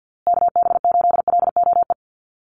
Morse code, heard as a keyed tone, about 50 wpm, 700 Hz. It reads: KB8LOE
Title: USA Call Signs Repeated 2x 50wpm